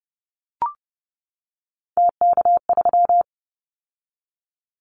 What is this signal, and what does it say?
Morse code, heard as a keyed tone, about 30 wpm, 700 Hz. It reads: TK3
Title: Callsign Prefixes for Top 100 DXCC - 30wpm